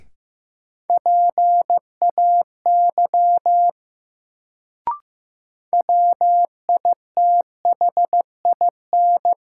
Morse code, heard as a keyed tone, about 15 wpm, 700 Hz. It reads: PAY WITHIN
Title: Sentences from Top 700 Words - New Word Drill - 15wpm